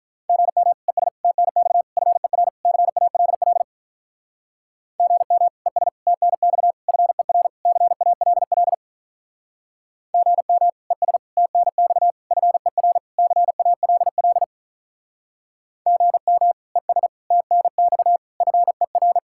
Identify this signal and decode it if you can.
Morse code, heard as a keyed tone, about 45 wpm, 700 Hz. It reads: GM ES TNX FER CALL GM ES TNX FER CALL GM ES TNX FER CALL GM ES TNX FER